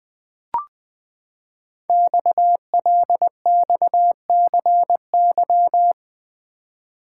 Morse code, heard as a keyed tone, about 20 wpm, 700 Hz. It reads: XLXCY
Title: Code Groups - 5 Letters - 20wpm